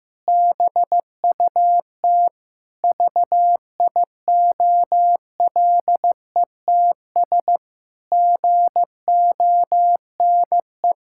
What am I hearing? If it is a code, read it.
Morse code, heard as a keyed tone, about 16 wpm, 700 Hz. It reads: BUT VIOLETS GONE
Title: Pangram Challenge v2 - 15wpm